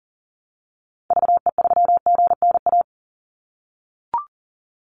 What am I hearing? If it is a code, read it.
Morse code, heard as a keyed tone, about 40 wpm, 700 Hz. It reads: VE3GNA
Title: Call Signs - 40wpm